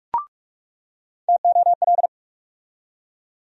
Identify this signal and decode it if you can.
Morse code, heard as a keyed tone, about 42 wpm, 700 Hz. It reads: TOL
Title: USA Call Sign Postfixes - 3 Letters 45wpm